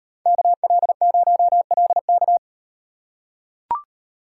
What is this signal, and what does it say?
Morse code, heard as a keyed tone, about 37 wpm, 700 Hz. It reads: KL0LK